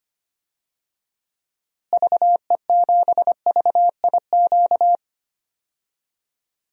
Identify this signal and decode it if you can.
Morse code, heard as a keyed tone, about 25 wpm, 700 Hz. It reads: VE7VIQ
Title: DXCC-100 Call Signs - 6 Characters - Repeated 2x - 25wpm